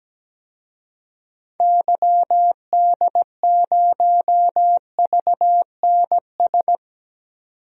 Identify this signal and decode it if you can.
Morse code, heard as a keyed tone, about 17 wpm, 700 Hz. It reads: YD0VNS
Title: DXCC-100 Call Signs - 6 Characters - 17wpm